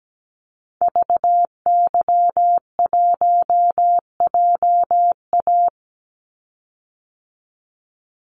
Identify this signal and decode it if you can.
Morse code, heard as a keyed tone, about 17 wpm, 700 Hz. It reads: VY1JA
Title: Call Signs Repeated 3x - 17wpm